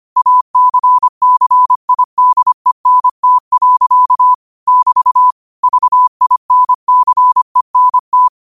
Morse code, heard as a keyed tone, about 25 wpm, 1000 Hz. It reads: ACCIDENT. <BT> VINCENT
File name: qotdfiles / 25wpm